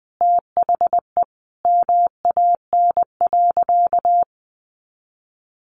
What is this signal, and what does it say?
Morse code, heard as a keyed tone, about 20 wpm, 700 Hz. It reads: THE MAN.
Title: Sentences from Top 1500 Words 20wpm